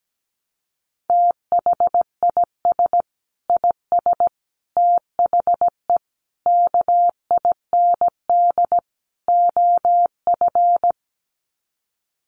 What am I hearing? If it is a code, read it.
Morse code, heard as a keyed tone, about 17 wpm, 700 Hz. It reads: THIS IS THE KIND OF
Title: Sets of 5 Words Encore 17wpm